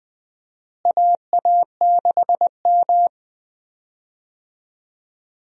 Morse code, heard as a keyed tone, about 21 wpm, 700 Hz. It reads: AA6M
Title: USA Call Signs - 4 Characters - Repeated 3x - 20wpm